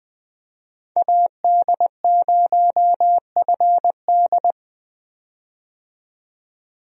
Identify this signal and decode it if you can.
Morse code, heard as a keyed tone, about 20 wpm, 700 Hz. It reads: AD0FD
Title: USA Call Signs - 5 Characters - 20wpm